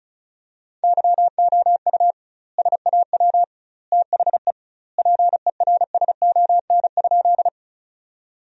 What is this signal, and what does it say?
Morse code, heard as a keyed tone, about 36 wpm, 700 Hz. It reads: YOU SAW THE PERSON?
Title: 2x Word-Spacing - Sentences from Top 400 Words - 35wpm